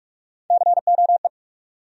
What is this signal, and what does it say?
Morse code, heard as a keyed tone, about 41 wpm, 700 Hz. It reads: COE
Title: USA Call Sign Postfixes - 3 Letters 45wpm